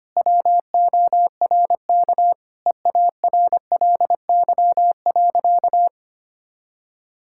Morse code, heard as a keyed tone, about 25 wpm, 700 Hz. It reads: WORK EARLY.
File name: practice / 25wpm / Sentences with 2-Word Phrases - 25wpm